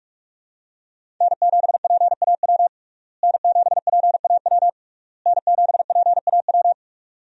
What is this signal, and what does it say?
Morse code, heard as a keyed tone, about 43 wpm, 700 Hz. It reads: N7PAW N7PAW N7PAW
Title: USA Call Signs - 5 Characters - Repeated 3x - 45wpm